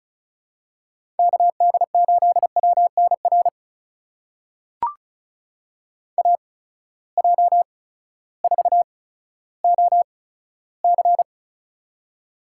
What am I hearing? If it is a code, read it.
Morse code, heard as a keyed tone, about 34 wpm, spaced as for 16 wpm, 700 Hz. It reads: KD8WNR AJ4OC
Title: USA Call Signs - 8x Character Spacing - 35wpm